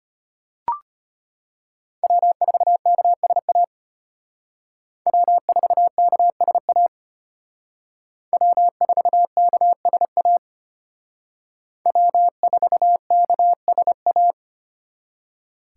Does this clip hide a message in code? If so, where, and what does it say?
Yes: Morse code from 2.0 to 14.3 s, heard as a keyed tone, about 33 wpm, 700 Hz. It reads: W4KSA W4KSA W4KSA W4KSA